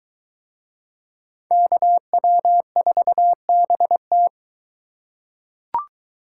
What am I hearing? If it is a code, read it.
Morse code, heard as a keyed tone, about 22 wpm, 700 Hz. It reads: KW4BT